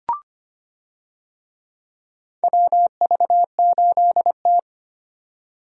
Morse code, heard as a keyed tone, about 25 wpm, 700 Hz. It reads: WV8T
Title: USA Call Signs - 4 Characters - 25wpm